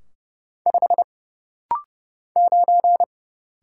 Morse code, heard as a keyed tone, about 30 wpm, 700 Hz. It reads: 5 9
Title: Single Letter-Number - Rapid-Fire 30wpm